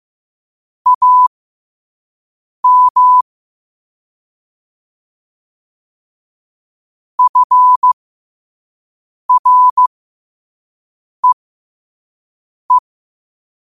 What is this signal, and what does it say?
Morse code, heard as a keyed tone, about 15 wpm, spaced as for 5 wpm, 1000 Hz. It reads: AM FREE
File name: qotdfiles / 5wpm